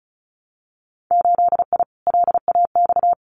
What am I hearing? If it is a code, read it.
Morse code, heard as a keyed tone, about 36 wpm, 700 Hz. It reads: 8I LAX
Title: Winter Field Day Exchanges Encore 35wpm